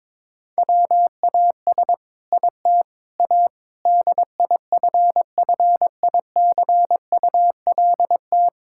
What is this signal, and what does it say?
Morse code, heard as a keyed tone, about 21 wpm, 700 Hz. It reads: WAS IT A DIFFICULT
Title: Sentences from Top 1000 Words - Review Part 6 - 22wpm